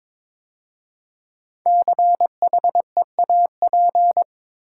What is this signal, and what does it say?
Morse code, heard as a keyed tone, about 22 wpm, 700 Hz. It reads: CHEAP
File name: 5 Letter Words - First 3 Letters Given - 22wpm